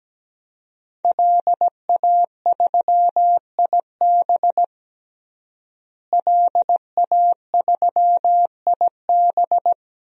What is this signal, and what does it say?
Morse code, heard as a keyed tone, about 17 wpm, 700 Hz. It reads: LA3IB LA3IB
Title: DXCC-100 Call Signs - 5 Characters - Repeated 3x - 17wpm